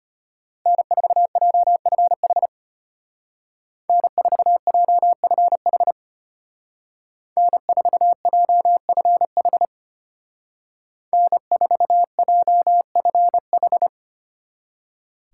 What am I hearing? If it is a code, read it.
Morse code, heard as a keyed tone, about 32 wpm, 700 Hz. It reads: N4JFH N4JFH N4JFH N4JFH